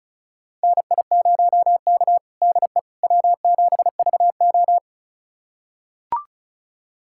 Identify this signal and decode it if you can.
Morse code, heard as a keyed tone, about 36 wpm, 700 Hz. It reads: NI0K DE W7VO
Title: USA Callsign DE Callsign - 4 Characters - 35wpm